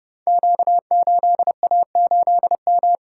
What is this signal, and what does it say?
Morse code, heard as a keyed tone, about 30 wpm, 700 Hz. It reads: Q8A8M
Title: Code Groups - 5 Letter:Numbers - 30wpm